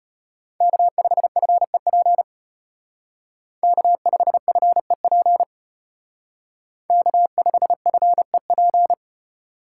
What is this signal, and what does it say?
Morse code, heard as a keyed tone, about 32 wpm, 700 Hz. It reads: K5FEP K5FEP K5FEP